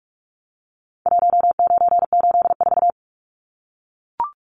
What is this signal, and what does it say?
Morse code, heard as a keyed tone, about 44 wpm, 700 Hz. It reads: 1984
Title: Numbers - Contest Style - 4-Digit 45wpm